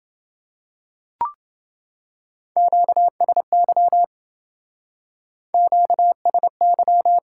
Morse code, heard as a keyed tone, about 27 wpm, 700 Hz. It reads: QSY QSY